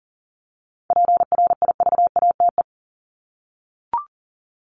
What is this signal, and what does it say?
Morse code, heard as a keyed tone, about 40 wpm, 700 Hz. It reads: PRIVATE
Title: Top 751-1000 Words 40wpm